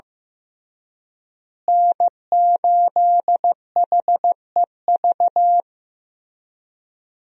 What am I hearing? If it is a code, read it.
Morse code, heard as a keyed tone, about 15 wpm, 700 Hz. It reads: N8HEV
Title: USA Call Signs - 5 Characters - Repeated 3x - 15wpm